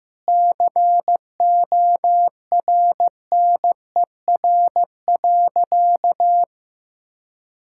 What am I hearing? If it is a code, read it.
Morse code, heard as a keyed tone, about 16 wpm, 700 Hz. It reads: CORNER.
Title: Sentences from Top 1600 Words 15wpm